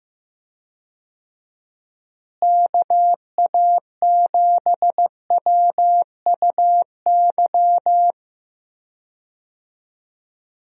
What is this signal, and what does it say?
Morse code, heard as a keyed tone, about 15 wpm, 700 Hz. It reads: KA7WUY
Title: USA Call Signs - 6 Characters - Repeated 2x - 15wpm